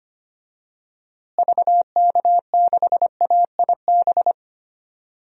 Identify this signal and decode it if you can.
Morse code, heard as a keyed tone, about 25 wpm, 700 Hz. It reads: VK6AIB
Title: DXCC-100 Call Signs - 6 Characters - Repeated 2x - 25wpm